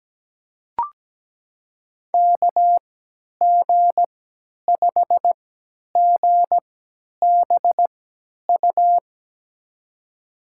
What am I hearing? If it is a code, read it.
Morse code, heard as a keyed tone, about 17 wpm, spaced as for 10 wpm, 700 Hz. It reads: KG5GBU
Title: USA Call Signs - 6 Characters - 3x Character Spacing - 17wpm